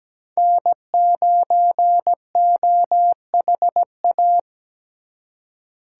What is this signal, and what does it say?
Morse code, heard as a keyed tone, about 20 wpm, 700 Hz. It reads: N9OHA
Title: Speed Racing - USA 1x3 Call Signs 17wpm